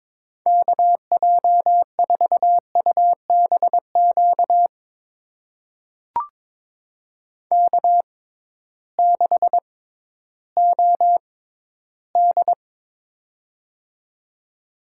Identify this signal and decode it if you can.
Morse code, heard as a keyed tone, about 22 wpm, spaced as for 13 wpm, 700 Hz. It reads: KJ4UBQ K6OD
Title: USA Call Signs - 6x Character Spacing - 22wpm